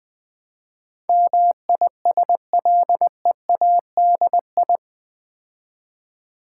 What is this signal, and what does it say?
Morse code, heard as a keyed tone, about 20 wpm, 700 Hz. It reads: MISLEADI
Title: Build A Word - 4 Letters At A Time - 10-Letter Words - 20wpm